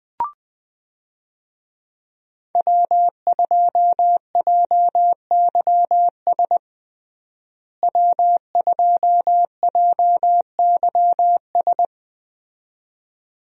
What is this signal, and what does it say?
Morse code, heard as a keyed tone, about 20 wpm, 700 Hz. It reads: W2JYS W2JYS
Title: USA Call Signs - 5 Characters - Repeated 2x - 20wpm